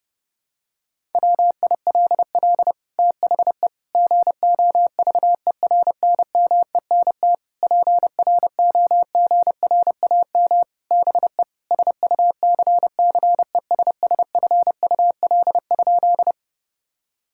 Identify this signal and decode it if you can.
Morse code, heard as a keyed tone, about 30 wpm, 700 Hz. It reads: WILL THE GOVERNMENT PROGRAM BE SUCCESSFUL?
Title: Sentences from Top 1000 Words - Review Part 5 - 30wpm